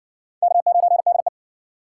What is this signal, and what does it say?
Morse code, heard as a keyed tone, about 60 wpm, 700 Hz. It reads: K9DE